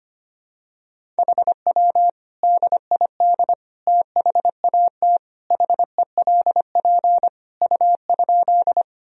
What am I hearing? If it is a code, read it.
Morse code, heard as a keyed tone, about 25 wpm, 700 Hz. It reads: HW DID THAT HELP U?
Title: Sentences from Top 200 Words and Abbreviations 25wpm